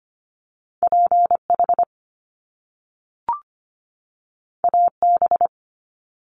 Morse code, heard as a keyed tone, about 25 wpm, 700 Hz. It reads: PH AB
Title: Top Two Letter Combinations in Words 25wpm